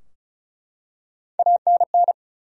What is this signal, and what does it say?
Morse code, heard as a keyed tone, about 34 wpm, 700 Hz. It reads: ANN
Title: USA Common Names - Max 5 Letters - 35wpm